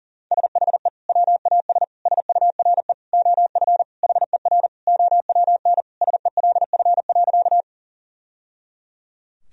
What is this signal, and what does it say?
Morse code, heard as a keyed tone, about 40 wpm, 700 Hz. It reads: SHE WAS SURE OF HER OWN SELF.
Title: Sentences from Top 1000 Words - Review Part 1 - 40wpm